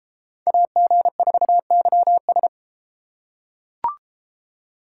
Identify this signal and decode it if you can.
Morse code, heard as a keyed tone, about 31 wpm, 700 Hz. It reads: AG4YS